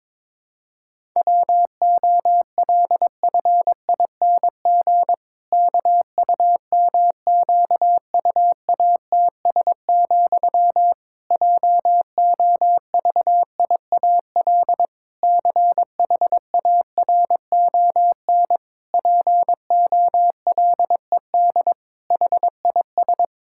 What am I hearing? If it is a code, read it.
Morse code, heard as a keyed tone, about 22 wpm, 700 Hz. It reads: WOLFING KUMQUATS, JOVIAL CHARON POLED HIS
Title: Pangram Challenge v2 - 22wpm